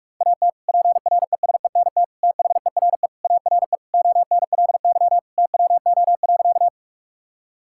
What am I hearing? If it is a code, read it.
Morse code, heard as a keyed tone, about 46 wpm, 700 Hz. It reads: AT PRESENT THERE ARE ONLY TWO.
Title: Sentences from Top 1000 Words - Review Part 5 - 45wpm